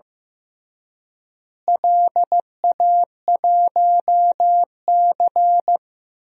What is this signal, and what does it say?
Morse code, heard as a keyed tone, about 15 wpm, 700 Hz. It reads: LA1C
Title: DXCC-100 Call Signs - 4 Characters - Repeated 3x - 15wpm